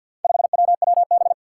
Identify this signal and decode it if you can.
Morse code, heard as a keyed tone, about 52 wpm, 700 Hz. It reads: 5WWB
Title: Four Challenging Letter-Number Combinations 50wpm